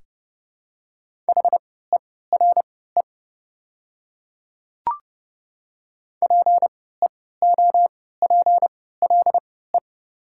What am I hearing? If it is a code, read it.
Morse code, heard as a keyed tone, about 30 wpm, spaced as for 17 wpm, 690 Hz. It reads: HERE PEOPLE
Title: 3x Character Spacing - Top 200 Words - 30wpm